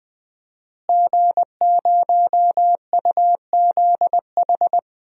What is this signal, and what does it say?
Morse code, heard as a keyed tone, about 20 wpm, 700 Hz. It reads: G0UZH
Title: DXCC-100 Call Signs - 5 Characters - Repeated 3x - 20wpm